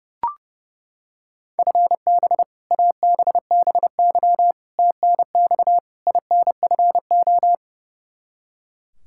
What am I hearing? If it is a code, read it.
Morse code, heard as a keyed tone, about 30 wpm, 700 Hz. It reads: FB ABBY TNX INFO
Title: Practice QSO - Exchange 3 - More Info - 30wpm